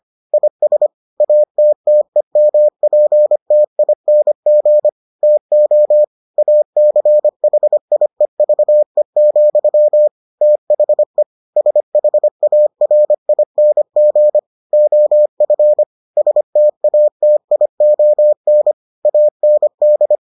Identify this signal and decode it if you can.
Morse code, heard as a keyed tone, about 25 wpm, 600 Hz. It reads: IS ATTEMPTING TO ACHIEVE, THE SHARING OF STATION AND